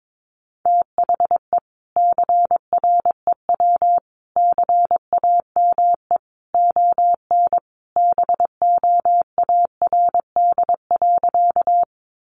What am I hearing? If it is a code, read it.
Morse code, heard as a keyed tone, about 22 wpm, 700 Hz. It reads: THE CREW CAME ON BOARD.
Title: Sentences from Top 1500 Words 22wpm